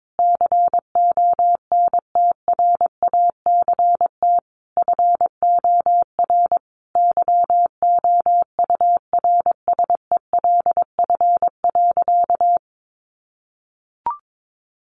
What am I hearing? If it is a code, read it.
Morse code, heard as a keyed tone, about 22 wpm, 700 Hz. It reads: CONTRACT FOR YOURSELF.
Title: Sentences from Top 1400 Words 22wpm